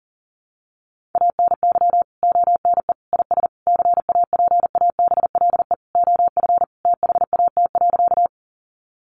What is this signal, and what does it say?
Morse code, heard as a keyed tone, about 40 wpm, 700 Hz. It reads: ANY ONE IS CAPABLE OF THAT.
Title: Sentences from Top 1200 Words 40wpm